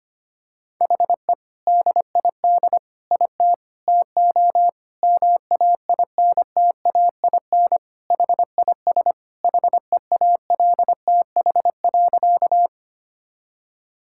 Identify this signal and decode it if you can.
Morse code, heard as a keyed tone, about 25 wpm, 700 Hz. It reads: HE DID IT TO MAINTAIN HIS HEALTH.
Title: Sentences from Top 1000 Words - Review Part 5 - 25wpm